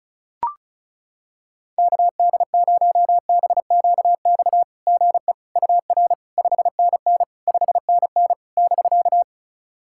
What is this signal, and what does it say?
Morse code, heard as a keyed tone, about 35 wpm, 660 Hz. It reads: KD0BQX GE UR 5NN 5NN <BK>
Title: SOTA QSO Practice 35wpm